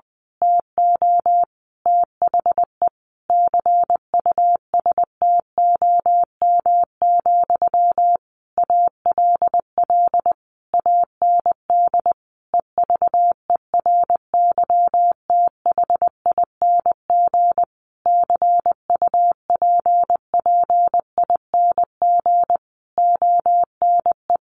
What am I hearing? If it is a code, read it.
Morse code, heard as a keyed tone, about 20 wpm, 700 Hz. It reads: TO THE CUSTOM, ALL AND EVERYTHING CUPPING ONE
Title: A Sound Of Thunder 20wpm